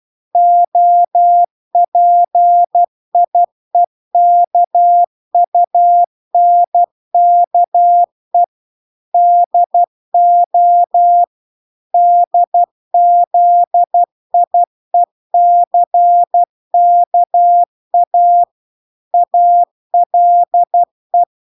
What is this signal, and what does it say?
Morse code, heard as a keyed tone, about 12 wpm, 700 Hz. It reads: OPIEKUNKE DO DZIECKA ALE